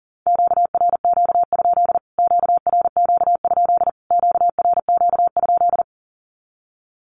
Morse code, heard as a keyed tone, about 40 wpm, 700 Hz. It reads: QRQ? QRQ? QRQ?
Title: Q-Codes Repeated 3x 40wpm